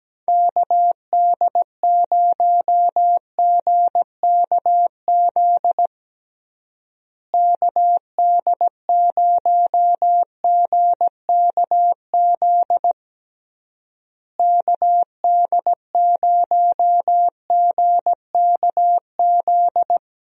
Morse code, heard as a keyed tone, about 17 wpm, 700 Hz. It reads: KD0GKZ KD0GKZ KD0GKZ
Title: USA Call Signs - 6 Characters - Repeated 3x - 17wpm